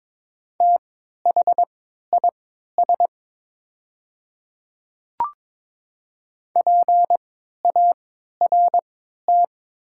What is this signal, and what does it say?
Morse code, heard as a keyed tone, about 22 wpm, spaced as for 12 wpm, 700 Hz. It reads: THIS PART
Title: 3x Character Spacing - Top 200 Words - 22wpm